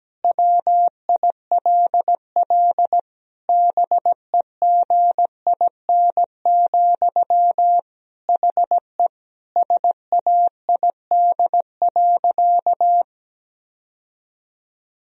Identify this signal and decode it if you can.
Morse code, heard as a keyed tone, about 17 wpm, 700 Hz. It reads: WILL BEGIN, HE SAID.
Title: Sentences from Top 1000 Words - Review Part 1 - 17wpm